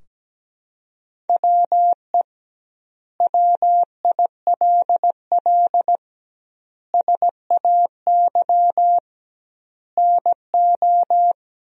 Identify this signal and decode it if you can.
Morse code, heard as a keyed tone, about 17 wpm, 700 Hz. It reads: WE WILL SAY NO
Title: 2x Word-Spacing - Sentences from Top 200 Words - 17wpm